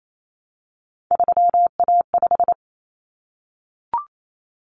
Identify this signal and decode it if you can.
Morse code, heard as a keyed tone, about 28 wpm, 700 Hz. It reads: 3A5
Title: Common Cut Numbers - 3-Digit 28wpm